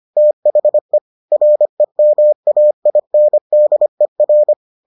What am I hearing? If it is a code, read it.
Morse code, heard as a keyed tone, about 25 wpm, 600 Hz. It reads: THE REMAINDER